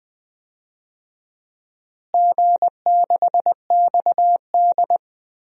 Morse code, heard as a keyed tone, about 20 wpm, 700 Hz. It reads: G6XD
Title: DXCC-100 Call Signs - 4 Characters - Repeated 3x - 20wpm